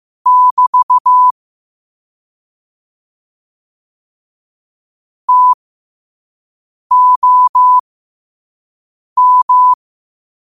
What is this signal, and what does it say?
Morse code, heard as a keyed tone, about 15 wpm, spaced as for 5 wpm, 1000 Hz. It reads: <BT> TOM
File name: qotdfiles / 5wpm